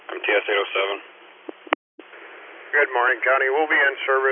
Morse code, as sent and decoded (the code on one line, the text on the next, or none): none